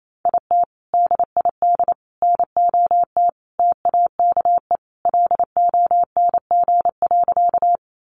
Morse code, heard as a keyed tone, about 28 wpm, 700 Hz. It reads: IT DID NOT TAKE LONG.
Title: Sentences from Top 600 Words 28wpm